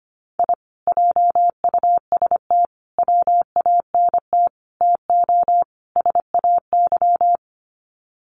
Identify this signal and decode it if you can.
Morse code, heard as a keyed tone, about 25 wpm, 700 Hz. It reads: I JUST WANT TO SAY
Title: Sets of 5 Words Encore 25wpm